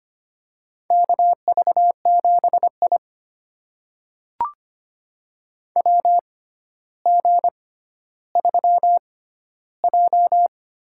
Morse code, heard as a keyed tone, about 25 wpm, spaced as for 13 wpm, 700 Hz. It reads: KV7I WG3J
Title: USA Call Signs - 4 Characters - 6x Character Spacing - 25wpm